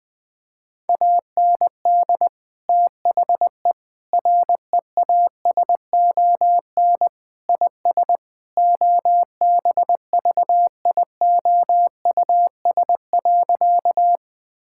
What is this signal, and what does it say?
Morse code, heard as a keyed tone, about 21 wpm, 700 Hz. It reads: AND THE REASON IS OBVIOUS.
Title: Sentences from Top 1600 Words 20wpm